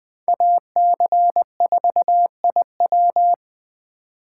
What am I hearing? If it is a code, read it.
Morse code, heard as a keyed tone, about 20 wpm, 700 Hz. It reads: AC4IW
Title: Speed Racing - USA 2x2 Call Signs 17wpm